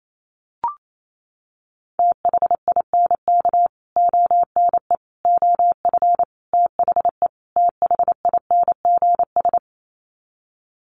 Morse code, heard as a keyed tone, about 28 wpm, 700 Hz. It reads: THINK ONE OF THE THINGS
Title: Sets of 5 Words Encore2 28wpm